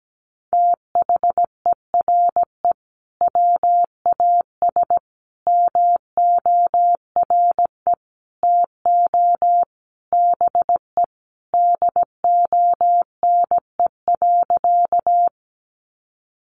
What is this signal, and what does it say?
Morse code, heard as a keyed tone, about 17 wpm, 700 Hz. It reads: THERE WAS MORE TO BE DONE.
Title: Sentences from Top 500 Words 17wpm